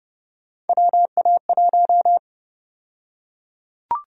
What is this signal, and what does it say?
Morse code, heard as a keyed tone, about 30 wpm, 700 Hz. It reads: WA1
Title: USA Call Sign Prefixes - Repeated 2x - 30wpm